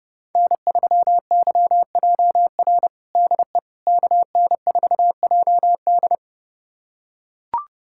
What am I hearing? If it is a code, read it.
Morse code, heard as a keyed tone, about 30 wpm, 700 Hz. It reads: N3YJR DE KN4JD
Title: USA Callsign DE Callsign - 5 Characters - 30wpm